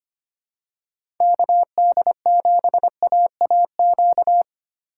Morse code, heard as a keyed tone, about 25 wpm, 700 Hz. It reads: KD7AAQ
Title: USA Call Signs - 6 Characters - 6x Character Spacing - 25wpm